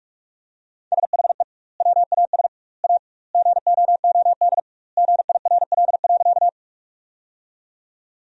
Morse code, heard as a keyed tone, about 46 wpm, 700 Hz. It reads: SHE WAS A GOOD GIRL.
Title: 2x Word-Spacing - Sentences from Top 400 Words - 45wpm